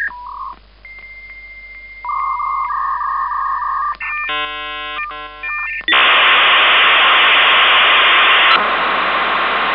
Dial-up modem handshake.